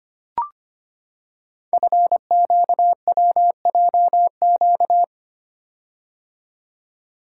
Morse code, heard as a keyed tone, about 25 wpm, 700 Hz. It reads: FQWJQ
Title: Code Groups - 5 Letters - 25wpm